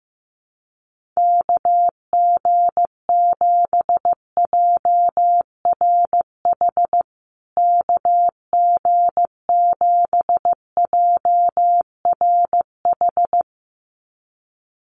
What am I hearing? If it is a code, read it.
Morse code, heard as a keyed tone, about 15 wpm, 700 Hz. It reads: KG7JRH KG7JRH
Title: USA Call Signs Repeated 2x 15wpm